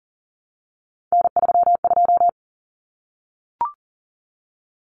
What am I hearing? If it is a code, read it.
Morse code, heard as a keyed tone, about 40 wpm, 700 Hz. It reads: N32
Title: Common Cut Numbers - 3-Digit 40wpm